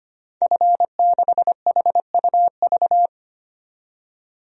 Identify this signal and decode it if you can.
Morse code, heard as a keyed tone, about 25 wpm, 700 Hz. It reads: F6HUV
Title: DXCC-100 Call Signs - 5 Characters - 25wpm